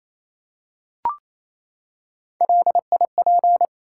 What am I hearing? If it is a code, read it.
Morse code, heard as a keyed tone, about 29 wpm, 700 Hz. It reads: LIP
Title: Three Letter Words - 28wpm